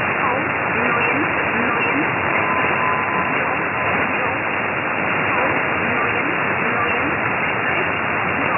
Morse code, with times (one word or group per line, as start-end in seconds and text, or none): none